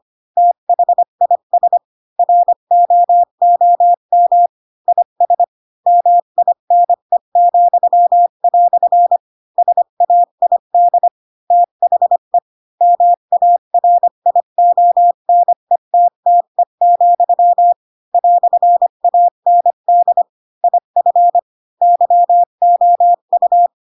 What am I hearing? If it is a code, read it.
Morse code, heard as a keyed tone, about 25 wpm, 700 Hz. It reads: THIS ROOM IS MINE," SAID THE MARIONETTE, "AND IF YOU
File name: Pinocchio / Pinocchio ch4 20 25 ik1hge